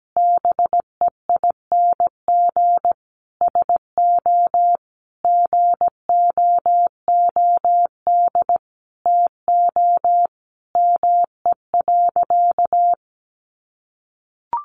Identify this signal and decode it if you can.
Morse code, heard as a keyed tone, about 17 wpm, 700 Hz. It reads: BEING SO GOOD TO ME.
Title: Sentences from Top 1400 Words 17wpm